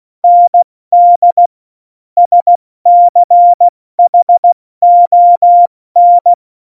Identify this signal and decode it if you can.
Morse code, heard as a keyed tone, about 16 wpm, spaced as for 14 wpm, 700 Hz. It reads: ND SCHON
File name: Morsekurs / Witze / Witz 26. Januar 2021, 70 F80